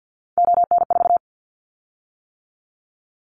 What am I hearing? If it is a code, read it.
Morse code, heard as a keyed tone, about 50 wpm, 700 Hz. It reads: ON4
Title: Callsign Prefixes for Top 25 DXCC - 50wpm